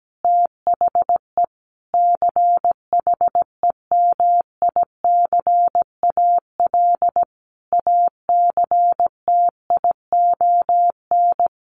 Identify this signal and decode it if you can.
Morse code, heard as a keyed tone, about 17 wpm, 700 Hz. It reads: THE CHEMICAL ACTION
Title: Sentences from Top 1200 Words 17wpm